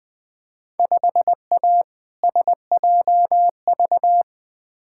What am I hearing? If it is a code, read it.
Morse code, heard as a keyed tone, about 20 wpm, 700 Hz. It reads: 5A SJV
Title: Field Day Exchanges - 20wpm